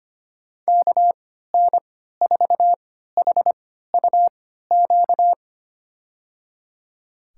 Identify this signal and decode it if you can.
Morse code, heard as a keyed tone, about 25 wpm, spaced as for 14 wpm, 700 Hz. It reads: KN4HUQ
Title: USA Call Signs - 6 Characters - 3x Character Spacing - 25wpm